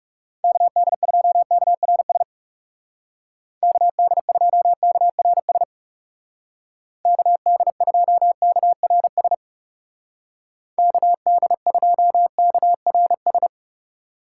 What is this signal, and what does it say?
Morse code, heard as a keyed tone, about 36 wpm, 700 Hz. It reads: KD2KRS KD2KRS KD2KRS KD2KRS
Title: Speed Racing - USA 2x3 Call Signs 30wpm